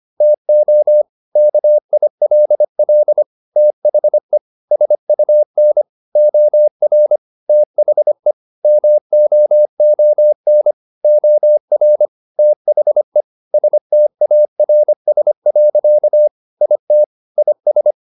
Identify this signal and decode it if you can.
Morse code, heard as a keyed tone, about 25 wpm, 600 Hz. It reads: TO KILL THE SUN OR THE MOON OR THE STARS. IT IS